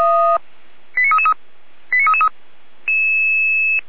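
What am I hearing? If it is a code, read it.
Five-tone selective call (ZVEI), heard as quick runs of five tones, tones of about 70 ms. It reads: 89333 89333
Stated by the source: ZVEI five-tone selective call received by SDR, from the Feuerwehr Orlamünde audio archive.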